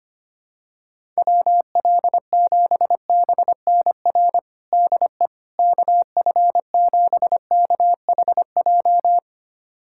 Morse code, heard as a keyed tone, about 25 wpm, 700 Hz. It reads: WL7BNR DE KF7KHJ
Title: USA Callsign DE Callsign - 6 Characters - 25wpm